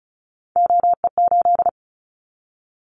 Morse code, heard as a keyed tone, about 36 wpm, 700 Hz. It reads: OE8
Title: Callsign Prefixes for Top 100 DXCC - 35wpm